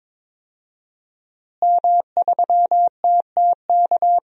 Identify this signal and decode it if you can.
Morse code, heard as a keyed tone, about 22 wpm, 700 Hz. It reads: M3TTK
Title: DXCC-100 Call Signs - 5 Characters - Repeated 2x - 22wpm